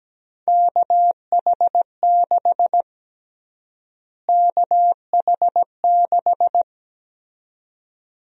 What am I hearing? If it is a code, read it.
Morse code, heard as a keyed tone, about 17 wpm, 700 Hz. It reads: KH6 KH6
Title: USA Call Sign Prefixes - Repeated 2x - 17wpm